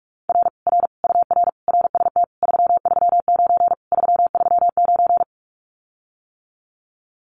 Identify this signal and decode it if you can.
Morse code, heard as a keyed tone, about 46 wpm, 700 Hz. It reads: R R UR RST 339 339
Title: Common QSO Phrases Encore 45wpm